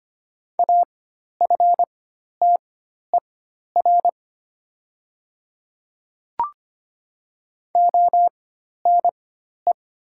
Morse code, heard as a keyed tone, about 25 wpm, spaced as for 12 wpm, 700 Hz. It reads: AFTER ONE
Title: 4x Character Spacing - Top 200 Words - 25wpm